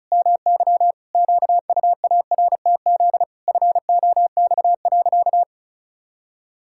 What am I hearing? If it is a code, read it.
Morse code, heard as a keyed tone, about 34 wpm, 700 Hz. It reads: MY QUARTZ FOX.
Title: Pangram Challenge v2 - 35wpm